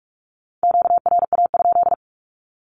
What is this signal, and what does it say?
Morse code, heard as a keyed tone, about 44 wpm, 700 Hz. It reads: QRA?
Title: Q-Codes Repeated 3x 45wpm